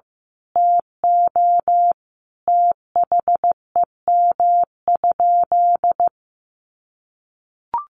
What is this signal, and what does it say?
Morse code, heard as a keyed tone, about 15 wpm, 700 Hz. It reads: TO THEM?
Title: Sentences from Top 1200 Words 15wpm